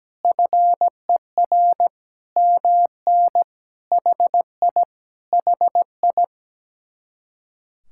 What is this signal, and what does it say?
Morse code, heard as a keyed tone, about 17 wpm, 700 Hz. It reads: FER MN HI HI
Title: Speed-Racing Common QSO Phrases - 17wpm